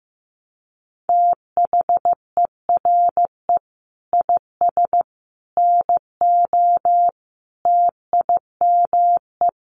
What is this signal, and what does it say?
Morse code, heard as a keyed tone, about 15 wpm, 700 Hz. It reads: THERE IS NO TIME
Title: Sentences from Top 1100 Words 15wpm